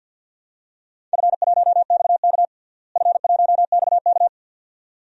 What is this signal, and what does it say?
Morse code, heard as a keyed tone, about 53 wpm, 700 Hz. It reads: F1XK F1XK
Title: DXCC-100 Call Signs - 4 Characters - Repeated 2x - 50wpm